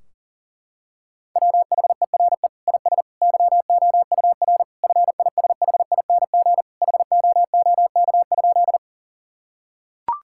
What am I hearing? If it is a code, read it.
Morse code, heard as a keyed tone, about 40 wpm, 700 Hz. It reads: WHERE IS YOUR FISHING HOOK?
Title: Sentences from Top 1600 Words 40wpm